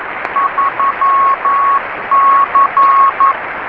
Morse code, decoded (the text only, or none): 3C